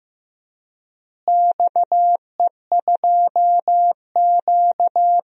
Morse code, heard as a keyed tone, about 15 wpm, 700 Hz. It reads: XE2Q
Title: DXCC-100 Call Signs - 4 Characters - Repeated 3x - 15wpm